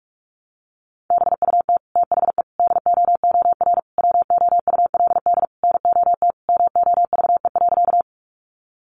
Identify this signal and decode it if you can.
Morse code, heard as a keyed tone, about 44 wpm, 700 Hz. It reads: BUT THE DOOR WOULD NOT MOVE.
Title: Sentences from Top 600 Words 45wpm